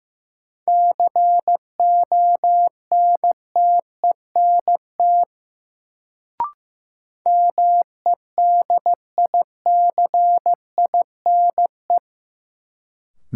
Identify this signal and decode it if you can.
Morse code, heard as a keyed tone, about 15 wpm, 700 Hz. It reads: CONTENT MEDICINE
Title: Sentences from Top 900 Words - New Word Drill - 15wpm